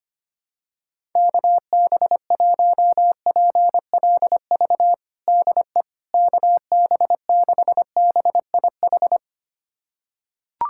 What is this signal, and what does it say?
Morse code, heard as a keyed tone, about 25 wpm, 700 Hz. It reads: KB1PLV DE KB6BIH
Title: USA Callsign DE Callsign - 6 Characters - 25wpm